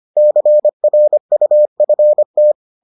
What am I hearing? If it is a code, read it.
Morse code, heard as a keyed tone, about 25 wpm, 600 Hz. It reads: CRUFT